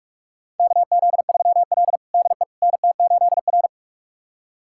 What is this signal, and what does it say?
Morse code, heard as a keyed tone, about 46 wpm, 700 Hz. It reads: KZ3L DE NT8R
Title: USA Callsign DE Callsign - 4 Characters - 45wpm